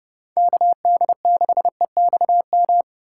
Morse code, heard as a keyed tone, about 30 wpm, 700 Hz. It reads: KD6EXM
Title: USA Call Signs - 6 Characters - 4x Character Spacing - 30wpm